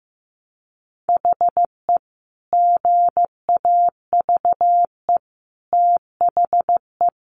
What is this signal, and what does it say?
Morse code, heard as a keyed tone, about 15 wpm, 700 Hz. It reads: HE GAVE THE
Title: Sentences from Top 1300 Words 15wpm